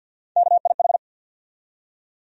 Morse code, heard as a keyed tone, about 48 wpm, 700 Hz. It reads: KIH
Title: USA Call Sign Postfixes - 3 Letters 50wpm